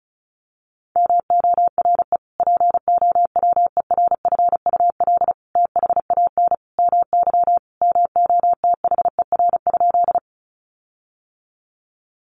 Morse code, heard as a keyed tone, about 34 wpm, 700 Hz. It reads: MORE POWERFUL THAN MY MOTHER?
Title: Sentences from Top 1300 Words 35wpm